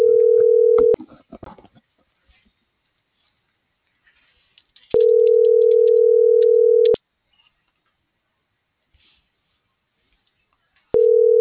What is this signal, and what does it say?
Ringback tone (North America), 440 + 480 Hz.